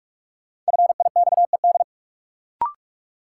Morse code, heard as a keyed tone, about 46 wpm, 700 Hz. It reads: FIXED
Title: Sentences from Top 1100 Words - New Word Drill - 45wpm